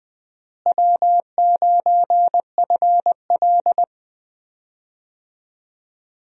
Morse code, heard as a keyed tone, about 21 wpm, 700 Hz. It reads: W9FL
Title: USA Call Signs - 4 Characters - Repeated 2x - 20wpm